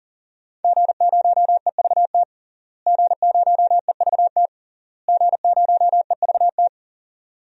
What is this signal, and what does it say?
Morse code, heard as a keyed tone, about 40 wpm, 700 Hz. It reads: G0EVT G0EVT G0EVT
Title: DXCC-100 Call Signs - 5 Characters - Repeated 3x - 40wpm